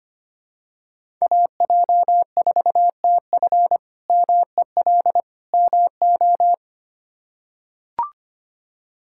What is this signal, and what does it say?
Morse code, heard as a keyed tone, about 25 wpm, 700 Hz. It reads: AJ4TF MEL MO
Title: K1USN SST QSO Practice - 25wpm